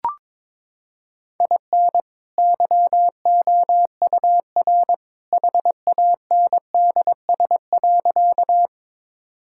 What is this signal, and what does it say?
Morse code, heard as a keyed tone, about 21 wpm, 700 Hz. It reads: IN YOUR HANDS.
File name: practice / 22wpm / Sentences with 2-Word Phrases - 22wpm